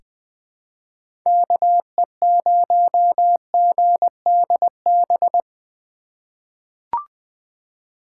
Morse code, heard as a keyed tone, about 21 wpm, 700 Hz. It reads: KE0GDB
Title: USA Call Signs - 6 Characters - 2x Character Spacing - 20wpm